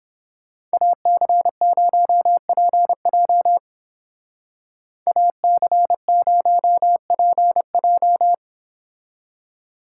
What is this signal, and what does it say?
Morse code, heard as a keyed tone, about 30 wpm, 700 Hz. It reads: AC0PJ AC0PJ